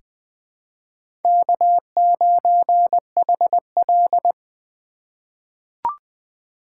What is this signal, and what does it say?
Morse code, heard as a keyed tone, about 21 wpm, 700 Hz. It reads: K9HL
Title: USA Call Signs - 4 Characters - 4x Character Spacing - 20wpm